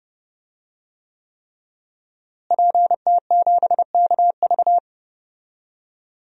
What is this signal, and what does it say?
Morse code, heard as a keyed tone, about 29 wpm, 700 Hz. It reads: PT7KV
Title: DXCC-100 Call Signs - 5 Characters - Repeated 3x - 30wpm